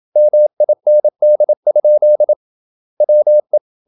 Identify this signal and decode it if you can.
Morse code, heard as a keyed tone, about 27 wpm, 600 Hz. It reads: MIND? WE
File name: Never 0027wpm